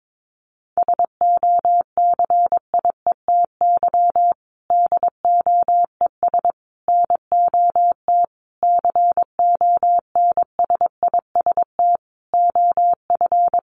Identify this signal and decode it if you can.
Morse code, heard as a keyed tone, about 22 wpm, 700 Hz. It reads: SOCIETY DOES NOT CONSIST OF
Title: Sentences from Top 1400 Words 22wpm